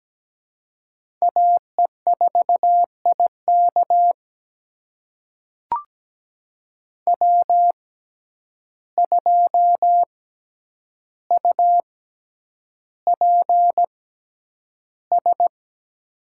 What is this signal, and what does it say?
Morse code, heard as a keyed tone, about 17 wpm, spaced as for 9 wpm, 700 Hz. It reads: AE4IK W2UPS
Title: USA Call Signs - 6x Character Spacing - 17wpm